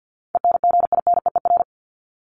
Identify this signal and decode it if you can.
Morse code, heard as a keyed tone, about 50 wpm, 700 Hz. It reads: ENGINEER
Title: ING Suffix - Top 100 Words with Base Words 50wpm